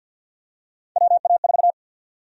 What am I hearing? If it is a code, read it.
Morse code, heard as a keyed tone, about 52 wpm, 700 Hz. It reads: WA4
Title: USA Call Sign Prefixes - Repeated 2x - 50wpm